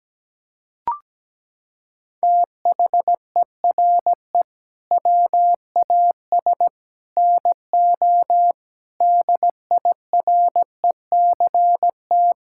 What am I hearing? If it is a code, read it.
Morse code, heard as a keyed tone, about 17 wpm, 700 Hz. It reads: THERE WAS NO DIRECT
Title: Sentences from Top 1000 Words - Review Part 6 - 17wpm